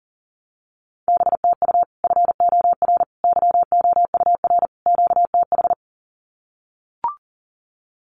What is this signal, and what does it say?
Morse code, heard as a keyed tone, about 40 wpm, 700 Hz. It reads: BTU FOR YOUR QTH
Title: Common QSO Phrases 40wpm